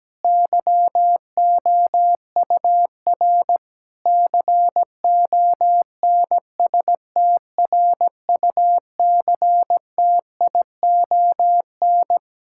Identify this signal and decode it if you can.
Morse code, heard as a keyed tone, about 17 wpm, 700 Hz. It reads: YOUR CONSTRUCTION
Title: Sentences from Top 1600 Words 17wpm